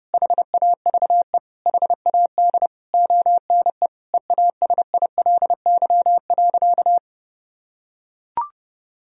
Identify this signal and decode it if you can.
Morse code, heard as a keyed tone, about 30 wpm, 700 Hz. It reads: HAVE HAD ONE EASILY.
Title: Sentences from Top 1000 Words - Review Part 3 - 30wpm